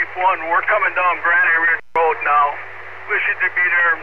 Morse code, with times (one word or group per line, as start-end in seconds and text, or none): none